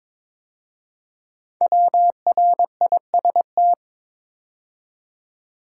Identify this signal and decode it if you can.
Morse code, heard as a keyed tone, about 22 wpm, 700 Hz. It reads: WRIST
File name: 5 Letter Words - First 3 Letters Given - 22wpm